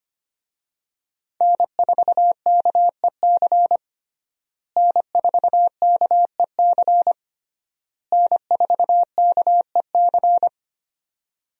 Morse code, heard as a keyed tone, about 25 wpm, 700 Hz. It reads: N4KEC N4KEC N4KEC
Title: USA Call Signs - 5 Characters - Repeated 3x - 25wpm